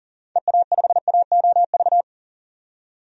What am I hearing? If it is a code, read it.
Morse code, heard as a keyed tone, about 38 wpm, 700 Hz. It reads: EA5AOV
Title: DXCC-100 Call Signs - 6 Characters - 40wpm